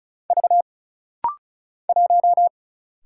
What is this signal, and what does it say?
Morse code, heard as a keyed tone, about 35 wpm, 700 Hz. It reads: V 1
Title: Single Letter-Number Alt - Rapid-Fire 35wpm